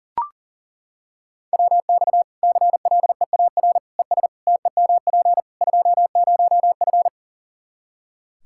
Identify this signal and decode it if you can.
Morse code, heard as a keyed tone, about 40 wpm, 700 Hz. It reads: WX CLEAR ES TEMP 20F
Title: K1USN SST QSO Practice 40wpm